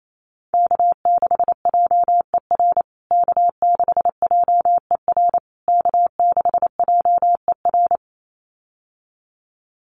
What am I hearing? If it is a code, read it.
Morse code, heard as a keyed tone, about 28 wpm, 700 Hz. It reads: K6JER K6JER K6JER
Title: USA Call Signs Repeated 3x 28wpm